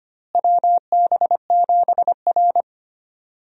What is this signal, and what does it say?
Morse code, heard as a keyed tone, about 25 wpm, 700 Hz. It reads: WB7R
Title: USA Call Signs - 4 Characters - Repeated 2x - 25wpm